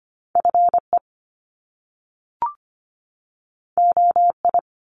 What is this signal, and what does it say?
Morse code, heard as a keyed tone, about 25 wpm, 700 Hz. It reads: FE OI
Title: Top Two Letter Combinations in Words 25wpm